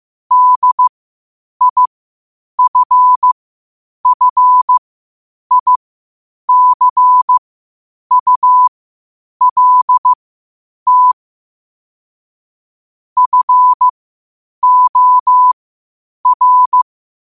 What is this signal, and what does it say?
Morse code, heard as a keyed tone, about 15 wpm, spaced as for 8 wpm, 1000 Hz. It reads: DIFFICULT FOR
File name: qotdfiles / 8wpm